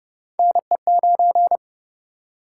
Morse code, heard as a keyed tone, about 28 wpm, 700 Hz. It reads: NE9